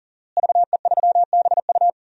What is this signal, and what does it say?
Morse code, heard as a keyed tone, about 40 wpm, 700 Hz. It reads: VE3BU
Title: DXCC-100 Call Signs - 5 Characters - Repeated 3x - 40wpm